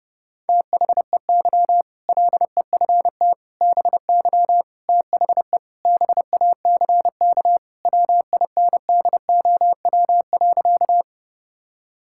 Morse code, heard as a keyed tone, about 29 wpm, 700 Hz. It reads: THEY LEFT BY THE BACK WINDOW.
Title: Sentences from Top 1600 Words 30wpm